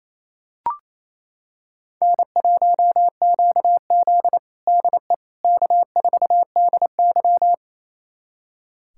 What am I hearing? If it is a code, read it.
Morse code, heard as a keyed tone, about 28 wpm, 700 Hz. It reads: N1QZ DE K4DY
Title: USA Callsign DE Callsign - 4 Characters - 28wpm